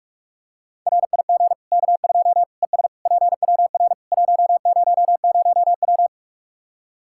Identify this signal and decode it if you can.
Morse code, heard as a keyed tone, about 42 wpm, 700 Hz. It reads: RIG K2 ES PWR 100W
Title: K1USN SST QSO Practice 45wpm